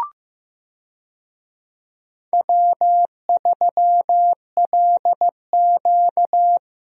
Morse code, heard as a keyed tone, about 15 wpm, 700 Hz. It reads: W3LQ
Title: USA Call Signs - 4 Characters - Repeated 2x - 15wpm